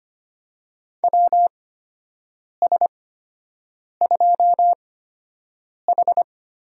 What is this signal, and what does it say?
Morse code, heard as a keyed tone, about 25 wpm, spaced as for 7 wpm, 700 Hz. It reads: WS2H
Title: USA Call Signs - 4 Characters - 8x Character Spacing - 25wpm